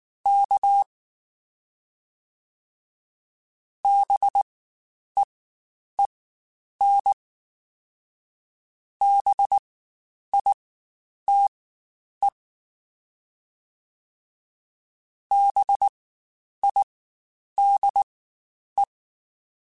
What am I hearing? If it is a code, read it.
Morse code, heard as a keyed tone, about 19 wpm, spaced as for 9 wpm, 800 Hz. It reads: K BEEN BITE BIDE
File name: Spaced Alphas / Spaced Alphas mp3s / Bs sp @18